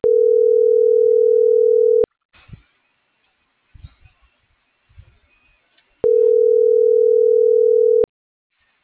Ringback tone (North America), 440 + 480 Hz.